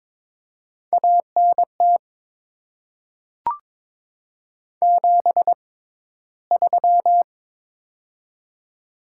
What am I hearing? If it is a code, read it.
Morse code, heard as a keyed tone, about 22 wpm, spaced as for 14 wpm, 700 Hz. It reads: ANT 73
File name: Top 20 QSO Elements - 6x Character Spacing - 22wpm